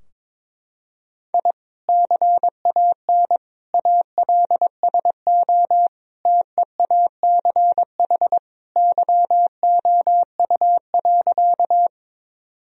Morse code, heard as a keyed tone, about 22 wpm, 700 Hz. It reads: I CAN ALSO TEACH YOU.
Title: Sentences from Top 1000 Words - Review Part 2 - 22wpm